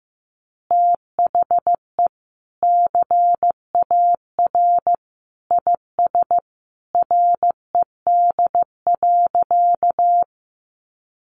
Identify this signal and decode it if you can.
Morse code, heard as a keyed tone, about 15 wpm, 700 Hz. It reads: THE CAR IS RED.
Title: Easy Phrases 15wpm